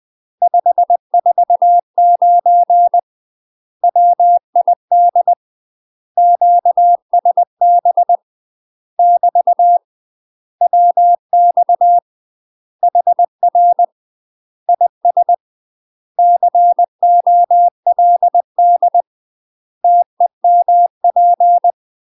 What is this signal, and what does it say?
Morse code, heard as a keyed tone, about 20 wpm, 700 Hz. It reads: 549 WID QSB <BT> WX HR IS COLD TEMP
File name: UK MASTER QSOs TRAINING / 20WPM / UK MASTER QSOs 0068